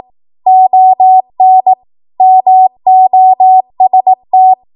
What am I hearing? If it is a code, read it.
Morse code, heard as a keyed tone, about 18 wpm, 750 Hz. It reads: ON MOST